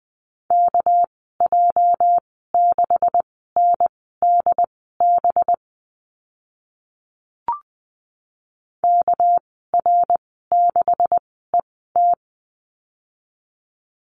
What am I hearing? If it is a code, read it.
Morse code, heard as a keyed tone, about 20 wpm, spaced as for 14 wpm, 700 Hz. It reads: KJ6NDB KR6ET
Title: 231 Lesson 37g - Letter x - Call Signs 20wpm